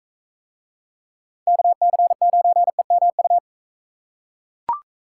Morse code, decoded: KC9EMU

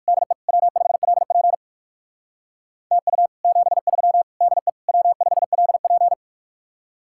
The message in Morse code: DE W5LP TU 73 DE W5LP

